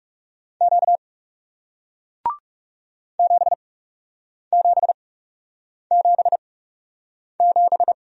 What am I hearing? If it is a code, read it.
Morse code, heard as a keyed tone, about 40 wpm, spaced as for 8 wpm, 700 Hz. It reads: Q 7777